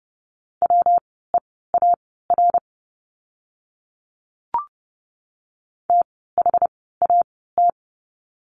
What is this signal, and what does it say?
Morse code, heard as a keyed tone, about 30 wpm, spaced as for 17 wpm, 700 Hz. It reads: WEAR THAT
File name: course / 30:10wpm / 115 Lesson 19f - Letter w - Words 30wpm